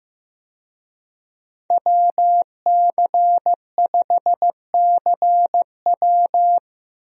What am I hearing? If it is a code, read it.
Morse code, heard as a keyed tone, about 15 wpm, 700 Hz. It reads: WC5CW
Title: USA Call Signs - 5 Characters - Repeated 3x - 15wpm